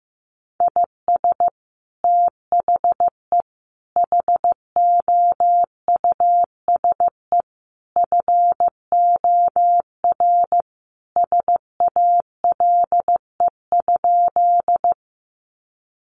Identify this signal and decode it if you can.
Morse code, heard as a keyed tone, about 15 wpm, 700 Hz. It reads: IS THE HOUSE FOR SALE?
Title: Sentences from Top 1400 Words 15wpm